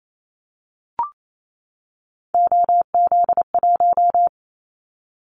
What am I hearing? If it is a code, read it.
Morse code, heard as a keyed tone, about 28 wpm, 700 Hz. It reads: OZ1
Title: Callsign Prefixes for Top 100 DXCC - 28wpm